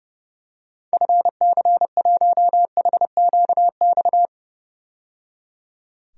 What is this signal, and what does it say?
Morse code, heard as a keyed tone, about 30 wpm, 700 Hz. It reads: FC1HQX
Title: DXCC-100 Call Signs - 6 Characters - 30wpm